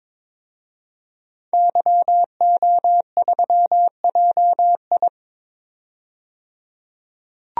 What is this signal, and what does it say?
Morse code, heard as a keyed tone, about 22 wpm, 700 Hz. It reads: YO3JI
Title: DXCC-100 Call Signs - 5 Characters - 22wpm